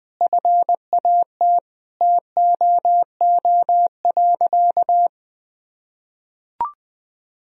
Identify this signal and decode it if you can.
Morse code, heard as a keyed tone, about 19 wpm, 700 Hz. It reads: FAT TOO.
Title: Sentences from Top 1000 Words - Review Part 4 - 20wpm